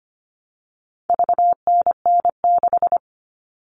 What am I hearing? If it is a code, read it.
Morse code, heard as a keyed tone, about 25 wpm, 700 Hz. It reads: VNN6
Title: All Cut Numbers - 4-Digit 25wpm